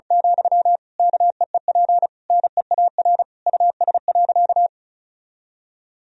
Morse code, heard as a keyed tone, about 34 wpm, 700 Hz. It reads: , KEEP NEAR US.
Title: Sentences from Top 1000 Words - Review Part 2 - 35wpm